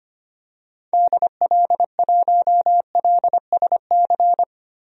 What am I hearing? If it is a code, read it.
Morse code, heard as a keyed tone, about 25 wpm, 700 Hz. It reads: DL1LSC
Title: DXCC-100 Call Signs - 6 Characters - Repeated 2x - 25wpm